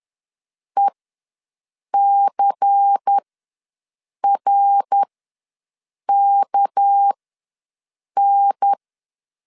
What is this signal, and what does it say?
Morse code, heard as a keyed tone, about 10 wpm, spaced as for 6 wpm, 800 Hz. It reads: ECRKN
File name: LEZIONI / 6 PKX0 / 6 PK ws 10wpm 3745